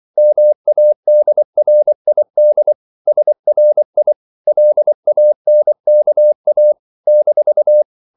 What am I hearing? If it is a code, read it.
Morse code, heard as a keyed tone, about 24 wpm, 600 Hz. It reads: MADRID SRI LANKA -
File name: CW / Practice / NationalCapitals24